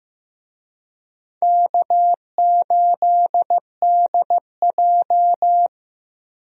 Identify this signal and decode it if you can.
Morse code, heard as a keyed tone, about 16 wpm, 700 Hz. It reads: K8DJ
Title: USA Call Signs - 4 Characters - Repeated 2x - 15wpm